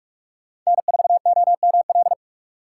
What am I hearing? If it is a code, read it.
Morse code, heard as a keyed tone, about 43 wpm, 700 Hz. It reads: N4OML
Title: USA Call Signs - 4x Character Spacing - 45wpm